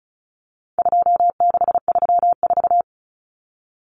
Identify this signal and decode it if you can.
Morse code, heard as a keyed tone, about 34 wpm, 700 Hz. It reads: 2634
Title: Numbers - 4-Digit 35wpm